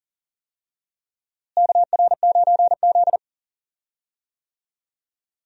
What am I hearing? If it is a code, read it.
Morse code, heard as a keyed tone, about 40 wpm, 700 Hz. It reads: KR9Z
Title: USA Call Signs - 4 Characters - Repeated 3x - 40wpm